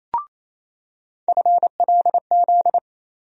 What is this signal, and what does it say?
Morse code, heard as a keyed tone, about 28 wpm, 700 Hz. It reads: FLZ
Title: USA Call Sign Postfixes - 3 Letters 28wpm